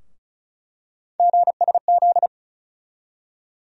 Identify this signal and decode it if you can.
Morse code, heard as a keyed tone, about 34 wpm, 700 Hz. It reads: GSZ
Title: USA Call Sign Postfixes - 3 Letters 35wpm